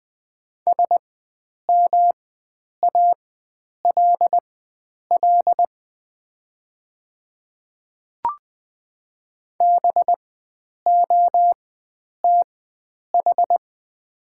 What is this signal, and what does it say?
Morse code, heard as a keyed tone, about 20 wpm, spaced as for 9 wpm, 700 Hz. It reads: SMALL BOTH
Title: 4x Character Spacing - Top 300 Words - 20wpm